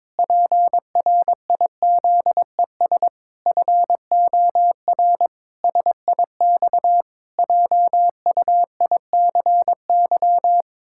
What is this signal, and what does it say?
Morse code, heard as a keyed tone, about 22 wpm, 700 Hz. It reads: PRIZES FOR SIX JUICY
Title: Pangram Challenge v2 - 22wpm